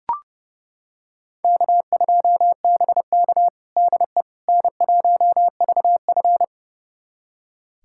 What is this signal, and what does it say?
Morse code, heard as a keyed tone, about 30 wpm, 700 Hz. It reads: K2BK DE N1VF
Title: USA Callsign DE Callsign - 4 Characters - 30wpm